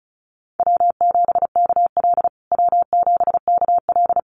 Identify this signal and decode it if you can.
Morse code, heard as a keyed tone, about 34 wpm, 700 Hz. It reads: W7KL W7KL
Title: USA Call Signs Repeated 2x 35wpm